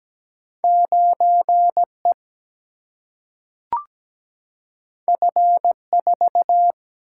Morse code, heard as a keyed tone, about 17 wpm, 700 Hz. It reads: 9 E F 4
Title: Single-Letter Single-Number 17wpm